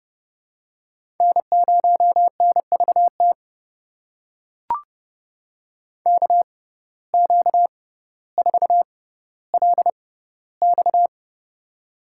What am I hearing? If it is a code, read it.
Morse code, heard as a keyed tone, about 30 wpm, spaced as for 15 wpm, 700 Hz. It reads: N0NVT KQ4LX
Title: USA Call Signs - 5 Characters - 6x Character Spacing - 30wpm